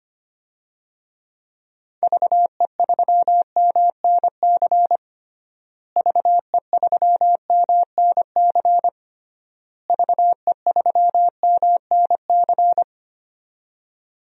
Morse code, heard as a keyed tone, about 25 wpm, 700 Hz. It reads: VE3MNC VE3MNC VE3MNC
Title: DXCC-100 Call Signs - 6 Characters - Repeated 3x - 25wpm